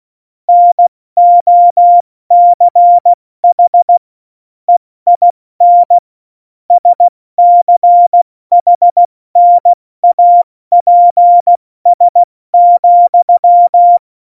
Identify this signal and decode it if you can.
Morse code, heard as a keyed tone, about 16 wpm, 700 Hz. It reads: NOCH EIN SCHNAPS,
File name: Morsekurs / Witze / Witz 03. November 2020, 70 F80